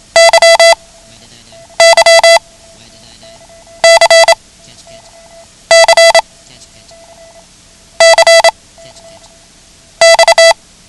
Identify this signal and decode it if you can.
Morse code, heard as a keyed tone, about 27 wpm, 670 Hz. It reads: Y Y C C C X
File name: ham radio / CodeQuick / Code Quick 09